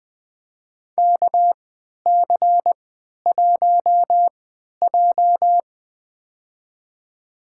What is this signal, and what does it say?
Morse code, heard as a keyed tone, about 20 wpm, spaced as for 11 wpm, 700 Hz. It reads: KC1J
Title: USA Call Signs - 4 Characters - 3x Character Spacing - 20wpm